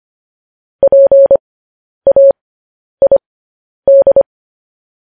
Morse code, heard as a keyed tone, about 25 wpm, spaced as for 10 wpm, 550 Hz. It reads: PAID
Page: News Headlines in Morse Code at 10 WPM